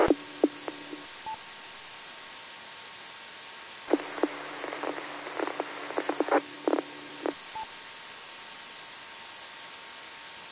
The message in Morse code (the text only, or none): none